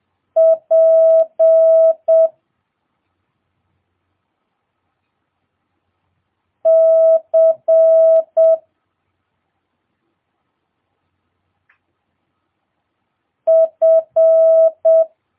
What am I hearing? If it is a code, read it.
Morse code, heard as a keyed tone, about 7 wpm, 650 Hz. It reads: P C F
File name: jh8cbh / 2022cw